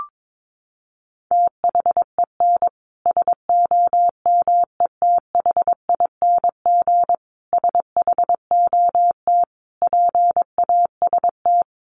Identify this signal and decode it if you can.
Morse code, heard as a keyed tone, about 22 wpm, 700 Hz. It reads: THEN SOMETHING SHOT PAST
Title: Sentences from Top 1100 Words 22wpm